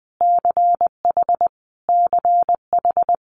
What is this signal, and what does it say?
Morse code, heard as a keyed tone, about 20 wpm, 700 Hz. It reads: CH CH
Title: Multi-letter Phonemes Repeated 3x 20wpm